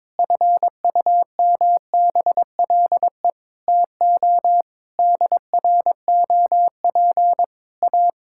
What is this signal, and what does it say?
Morse code, heard as a keyed tone, about 22 wpm, 700 Hz. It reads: FUMBLE TO DROP A
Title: Pangram Challenge v2 - 22wpm